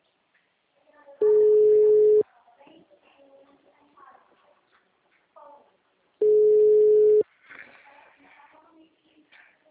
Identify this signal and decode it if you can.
Ringback tone (Europe, 425 Hz), with speech.
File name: out- -2011- - - .458